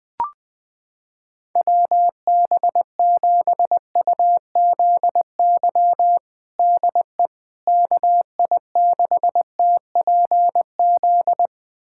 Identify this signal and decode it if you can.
Morse code, heard as a keyed tone, about 21 wpm, 700 Hz. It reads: WB7UZY DE KI6TPZ
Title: USA Callsign DE Callsign - 6 Characters - 20wpm